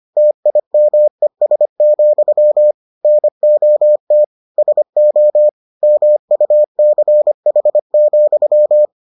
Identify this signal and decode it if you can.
Morse code, heard as a keyed tone, about 25 wpm, 600 Hz. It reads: TIMES, NOT SO MUCH,